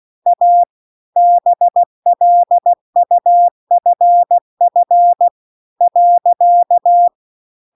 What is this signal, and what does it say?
Morse code, heard as a keyed tone, about 16 wpm, 700 Hz. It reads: A BLUFF .